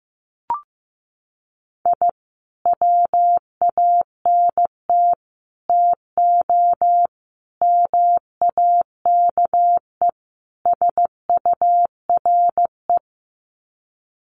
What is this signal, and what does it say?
Morse code, heard as a keyed tone, about 15 wpm, 700 Hz. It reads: I WANT TO MAKE SURE
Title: Sets of 5 Words Encore2 15wpm